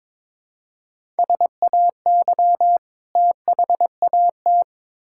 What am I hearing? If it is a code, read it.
Morse code, heard as a keyed tone, about 22 wpm, 700 Hz. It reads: SAY THAT
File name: Build A Phrase - 2-Word Phrases - 22wpm